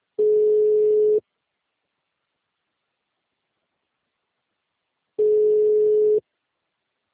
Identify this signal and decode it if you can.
Ringback tone (Europe, 425 Hz).